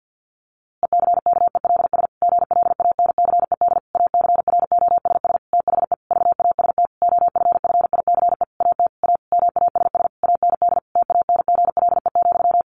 Morse code, heard as a keyed tone, about 50 wpm, 700 Hz. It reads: ECKELS GLANCED ACROSS THE VAST OFFICE AT A MASS AND TANGLE,
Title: A Sound Of Thunder 50wpm